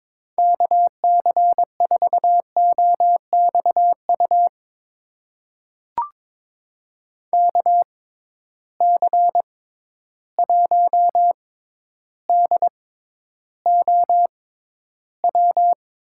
Morse code, heard as a keyed tone, about 22 wpm, spaced as for 11 wpm, 700 Hz. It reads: KC4OXU KC1DOW
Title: USA Call Signs - 6 Characters - 6x Character Spacing - 22wpm